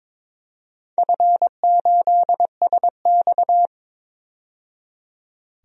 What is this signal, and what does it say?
Morse code, heard as a keyed tone, about 22 wpm, 700 Hz. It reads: F8SX
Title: DXCC-100 Call Signs - 4 Characters - 22wpm